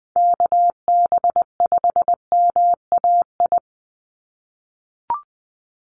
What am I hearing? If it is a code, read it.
Morse code, heard as a keyed tone, about 20 wpm, 700 Hz. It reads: KB5MAI
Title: USA Call Signs 20wpm